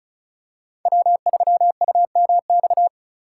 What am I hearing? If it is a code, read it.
Morse code, heard as a keyed tone, about 34 wpm, 700 Hz. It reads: W3UMX
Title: USA Call Signs - 5 Characters - 3x Character Spacing - 35wpm